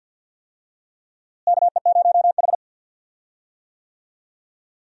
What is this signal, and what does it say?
Morse code, heard as a keyed tone, about 47 wpm, 700 Hz. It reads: KE0H
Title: USA Call Signs - 4 Characters - Repeated 2x - 50wpm